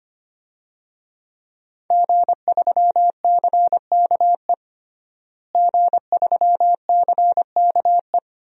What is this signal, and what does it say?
Morse code, heard as a keyed tone, about 25 wpm, 700 Hz. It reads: G3CKE G3CKE
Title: DXCC-100 Call Signs - 5 Characters - Repeated 2x - 25wpm